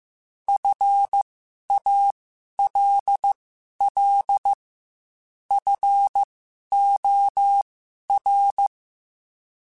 Morse code, heard as a keyed tone, about 15 wpm, spaced as for 11 wpm, 800 Hz. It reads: FALL FOR
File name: Quotes / Quotes-mp3s / StandFor @13